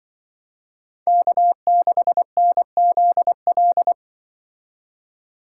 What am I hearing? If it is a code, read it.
Morse code, heard as a keyed tone, about 24 wpm, 700 Hz. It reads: K6NZL
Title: Speed Racing - USA 1x3 Call Signs 20wpm